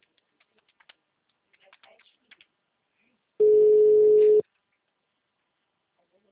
Ringback tone (Europe, 425 Hz).